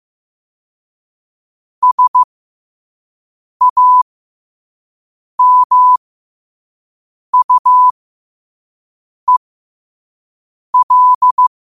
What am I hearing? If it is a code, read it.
Morse code, heard as a keyed tone, about 15 wpm, spaced as for 5 wpm, 1000 Hz. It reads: SAMUEL
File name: qotdfiles / 5wpm